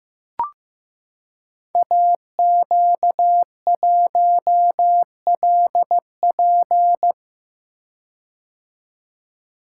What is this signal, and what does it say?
Morse code, heard as a keyed tone, about 15 wpm, 700 Hz. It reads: AQ1LP
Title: Code Groups - 5 Letter:Numbers - 15wpm